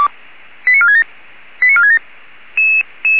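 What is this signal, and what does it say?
Five-tone selective call (ZVEI), heard as quick runs of five tones, tones of about 80 ms. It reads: 89467 89467
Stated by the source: ZVEI five-tone selective call received on SDR.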